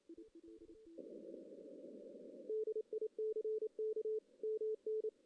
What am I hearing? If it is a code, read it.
Morse code, heard as a keyed tone, about 27 wpm, 440 Hz. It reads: DICK MN
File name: 2016 JAN CW NAQP / 2016 Jan naqp k5ot - 0-2